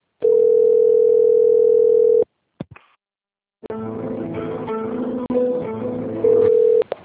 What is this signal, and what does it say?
Ringback tone (North America), 440 + 480 Hz.
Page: Jacarei - SP